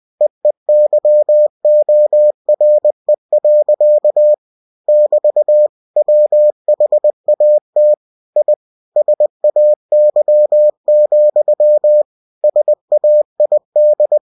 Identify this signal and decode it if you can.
Morse code, heard as a keyed tone, about 20 wpm, 600 Hz. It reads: EEYORE. <BT> WHAT I SAY, SAID
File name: cw-books / winnie-the-pooh / winnie-20wpm-page-0079